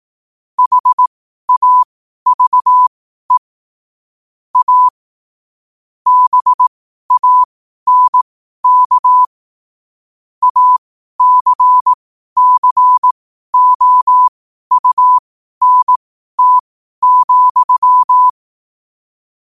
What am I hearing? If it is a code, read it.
Morse code, heard as a keyed tone, about 18 wpm, spaced as for 12 wpm, 1000 Hz. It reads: HAVE A BANK ACCOUNT,
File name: qotdfiles / 12wpm